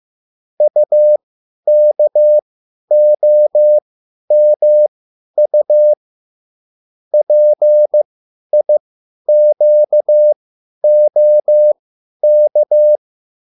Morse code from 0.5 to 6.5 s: UKOMU